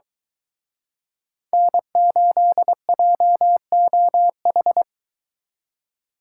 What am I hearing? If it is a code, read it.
Morse code, heard as a keyed tone, about 22 wpm, 700 Hz. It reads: N8JOH